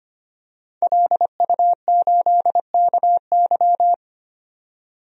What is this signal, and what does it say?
Morse code, heard as a keyed tone, about 25 wpm, 700 Hz. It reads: LU8KY
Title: DXCC-100 Call Signs - 5 Characters - 25wpm